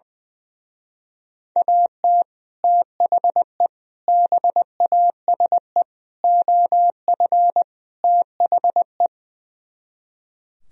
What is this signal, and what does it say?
Morse code, heard as a keyed tone, about 20 wpm, 700 Hz. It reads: AT THE BASE OF THE
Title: Speed-Racing - Sets of 5 Words 20wpm